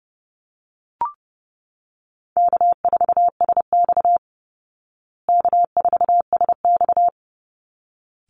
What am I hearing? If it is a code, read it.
Morse code, heard as a keyed tone, about 30 wpm, 700 Hz. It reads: K4SX K4SX
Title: Easy Call Signs Repeated 2x - 30wpm